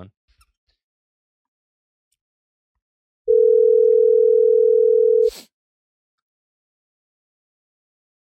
Ringback tone (North America), 440 + 480 Hz.